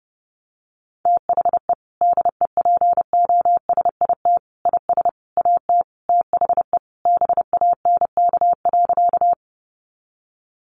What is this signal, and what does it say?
Morse code, heard as a keyed tone, about 30 wpm, 700 Hz. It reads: THE DEPOSIT IS AT THE BANK.
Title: Sentences from Top 1300 Words 30wpm